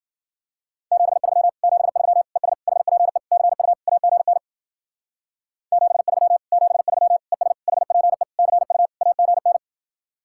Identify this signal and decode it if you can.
Morse code, heard as a keyed tone, about 52 wpm, 700 Hz. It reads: 73 73 ES HPE CU AGN 73 73 ES HPE CU AGN